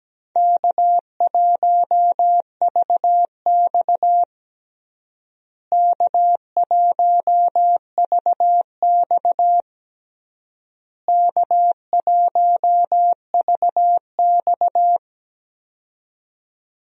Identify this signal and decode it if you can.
Morse code, heard as a keyed tone, about 17 wpm, 700 Hz. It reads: K1VX K1VX K1VX
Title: USA Call Signs - 4 Characters - Repeated 3x - 17wpm